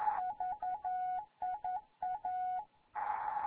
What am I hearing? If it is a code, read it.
Morse code, heard as a keyed tone, about 11 wpm, 740 Hz. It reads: VIA
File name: Morze-HA2MN